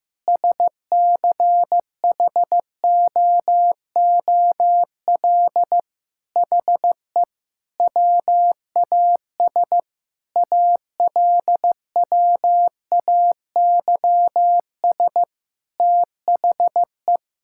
Morse code, heard as a keyed tone, about 15 wpm, 700 Hz. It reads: SCHOOL HE WAS ALWAYS THE
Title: Sentences from Top 1000 Words - Review Part 5 - 15wpm